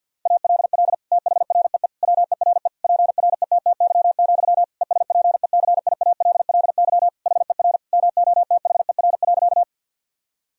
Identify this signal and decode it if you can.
Morse code, heard as a keyed tone, about 46 wpm, 700 Hz. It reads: ALL THREE WERE PRETTY, ESPECIALLY HER MOTHER.
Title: Sentences from Top 1000 Words - Review Part 4 - 50wpm